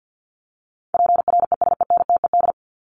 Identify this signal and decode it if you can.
Morse code, heard as a keyed tone, about 50 wpm, 700 Hz. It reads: PRESENTED
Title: ED Suffix - Top 100 Words with Base Words 50wpm